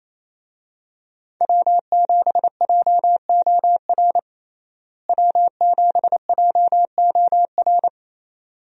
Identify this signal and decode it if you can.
Morse code, heard as a keyed tone, about 28 wpm, 700 Hz. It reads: W7JOR W7JOR
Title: USA Call Signs - 5 Characters - Repeated 2x - 28wpm